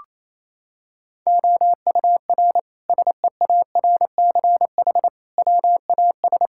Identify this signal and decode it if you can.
Morse code, heard as a keyed tone, about 28 wpm, 700 Hz. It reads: OUR SEARCH WAS
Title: Sentences from Top 1000 Words - Review Part 5 - 28wpm